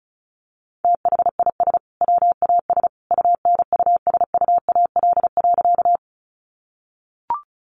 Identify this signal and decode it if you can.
Morse code, heard as a keyed tone, about 34 wpm, 700 Hz. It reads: THIS WAS UNUSUAL.
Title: Sentences from Top 1400 Words 35wpm